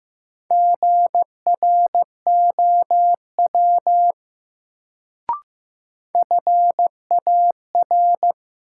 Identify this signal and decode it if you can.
Morse code, heard as a keyed tone, about 15 wpm, 700 Hz. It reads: GROW FAR
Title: Sentences from Top 300 Words - New Word Drill - 15wpm